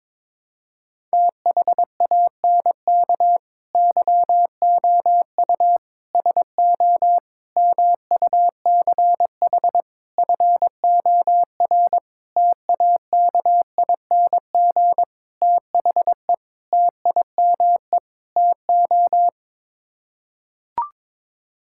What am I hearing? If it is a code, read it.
Morse code, heard as a keyed tone, about 22 wpm, 700 Hz. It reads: THANK YOU SO MUCH FOR TAKING THE TIME TO
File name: Sets of 9 Words - 22wpm